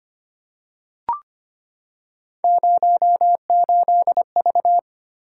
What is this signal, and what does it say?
Morse code, heard as a keyed tone, about 25 wpm, 700 Hz. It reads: 08V
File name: Three Letter-Number Combinations - 25wpm